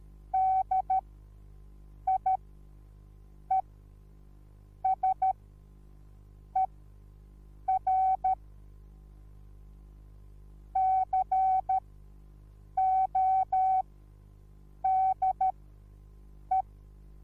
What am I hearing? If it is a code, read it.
Morse code, heard as a keyed tone, about 13 wpm, spaced as for 6 wpm, 760 Hz. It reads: DIESER CODE